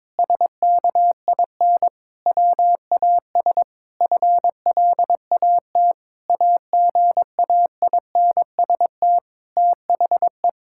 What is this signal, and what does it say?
Morse code, heard as a keyed tone, about 22 wpm, 700 Hz. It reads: SKIN WAS FLAT AGAINST THE
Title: Sentences from Top 1600 Words 22wpm